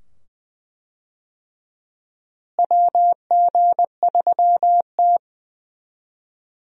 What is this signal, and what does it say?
Morse code, heard as a keyed tone, about 20 wpm, 700 Hz. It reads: WG3T
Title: USA Call Signs - 4 Characters - Repeated 2x - 20wpm